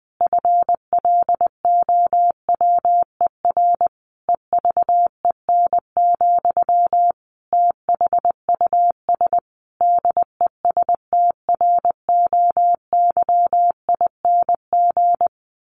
Morse code, heard as a keyed tone, about 20 wpm, 700 Hz. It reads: FLOWER EVEN, THUS DESTROYING
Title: A Sound Of Thunder 20wpm